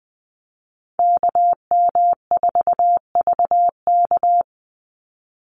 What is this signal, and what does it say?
Morse code, heard as a keyed tone, about 20 wpm, 700 Hz. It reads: KM4VK
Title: USA Call Signs 20wpm